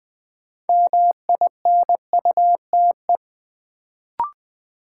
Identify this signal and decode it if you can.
Morse code, heard as a keyed tone, about 20 wpm, 700 Hz. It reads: MINUTE
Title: Sentences from Top 500 Words - New Word Drill - 20wpm